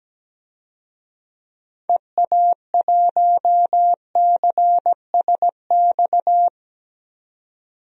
Morse code, heard as a keyed tone, about 17 wpm, 700 Hz. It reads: EA1CSX
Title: DXCC-100 Call Signs - 6 Characters - 17wpm